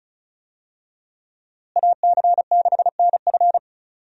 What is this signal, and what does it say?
Morse code, heard as a keyed tone, about 36 wpm, 700 Hz. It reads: AC6NF
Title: USA Call Signs - 5 Characters - Repeated 2x - 35wpm